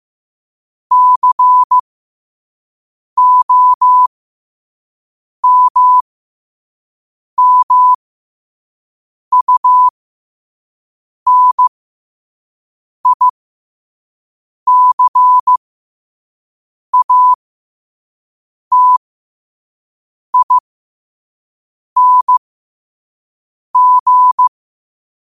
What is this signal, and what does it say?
Morse code, heard as a keyed tone, about 15 wpm, spaced as for 5 wpm, 1000 Hz. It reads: COMMUNICATING
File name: qotdfiles / 5wpm